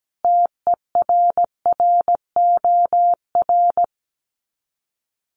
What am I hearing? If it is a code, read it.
Morse code, heard as a keyed tone, about 17 wpm, 700 Hz. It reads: TERROR
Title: Sets of 5 Words Encore2 17wpm